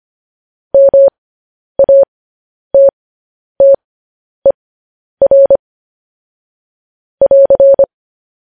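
Morse code, heard as a keyed tone, about 25 wpm, spaced as for 10 wpm, 550 Hz. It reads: MATTER <AR>
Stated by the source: Morse code news headlines practice at 10 wpm.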